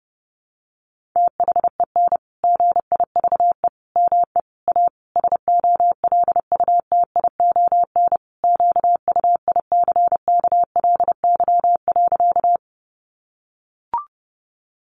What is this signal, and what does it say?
Morse code, heard as a keyed tone, about 30 wpm, 700 Hz. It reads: THEN GIVE ME A SOLUTION QUICKLY.
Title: Sentences from Top 1300 Words 30wpm